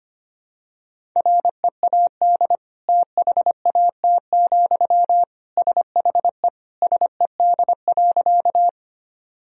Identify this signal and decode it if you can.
Morse code, heard as a keyed tone, about 25 wpm, 700 Hz. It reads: READ THAT, SHE SED.
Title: Sentences from Top 200 Words and Abbreviations 25wpm